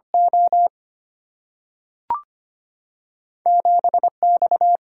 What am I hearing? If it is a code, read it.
Morse code, heard as a keyed tone, about 25 wpm, 700 Hz. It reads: O 7 X
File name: Two Letter-Number Combinations - 25wpm